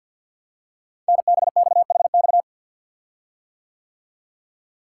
Morse code, heard as a keyed tone, about 51 wpm, 700 Hz. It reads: NBXHX
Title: Code Groups - 5 Letters - 50wpm